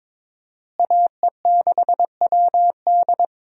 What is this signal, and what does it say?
Morse code, heard as a keyed tone, about 22 wpm, 700 Hz. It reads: AE6WD
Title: USA Call Signs - 5 Characters - Repeated 3x - 22wpm